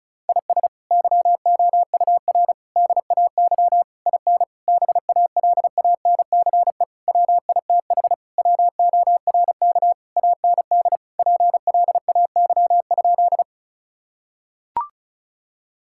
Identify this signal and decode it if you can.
Morse code, heard as a keyed tone, about 34 wpm, 700 Hz. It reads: IS YOUR DAY IN BALANCE WITH WORK AND PLAY?
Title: Sentences from Top 1000 Words - Review Part 6 - 35wpm